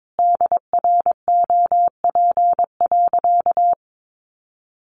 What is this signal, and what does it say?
Morse code, heard as a keyed tone, about 22 wpm, 700 Hz. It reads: DROP.
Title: Sentences from Top 1400 Words 22wpm